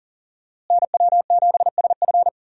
Morse code, heard as a keyed tone, about 40 wpm, 700 Hz. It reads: NW7SF
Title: USA Call Signs - 5 Characters - Repeated 3x - 40wpm